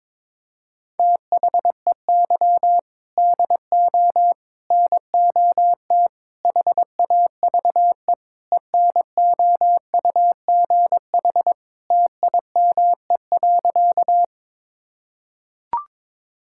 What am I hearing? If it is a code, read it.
Morse code, heard as a keyed tone, about 22 wpm, 700 Hz. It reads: THEY DO NOT HAVE ENOUGH TIME.
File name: practice / 22wpm / Sentences with 4-Word Phrases - 22wpm